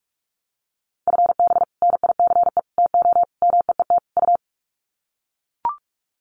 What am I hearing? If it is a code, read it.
Morse code, heard as a keyed tone, about 44 wpm, 700 Hz. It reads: FB NICE TO MEET U
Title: Common QSO Phrases 45wpm